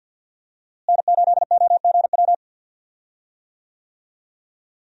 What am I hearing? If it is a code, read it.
Morse code, heard as a keyed tone, about 46 wpm, 700 Hz. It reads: N8OGW
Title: USA Call Signs - 5 Characters - Repeated 2x - 50wpm